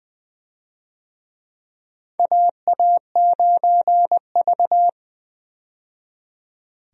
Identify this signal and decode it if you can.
Morse code, heard as a keyed tone, about 20 wpm, 700 Hz. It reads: AA9V
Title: USA Call Signs - 4 Characters - 20wpm